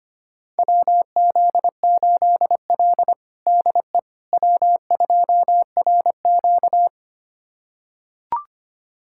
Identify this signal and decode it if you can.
Morse code, heard as a keyed tone, about 25 wpm, 700 Hz. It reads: WZ8L DE W2RQ
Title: USA Callsign DE Callsign - 4 Characters - 25wpm